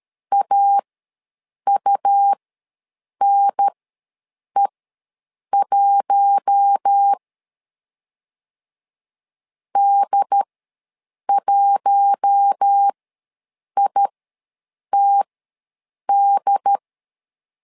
Morse code, heard as a keyed tone, about 13 wpm, spaced as for 7 wpm, 800 Hz. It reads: AUNE1 D1ITD
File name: LEZIONI / 4 Q1 ws 12wpm 2129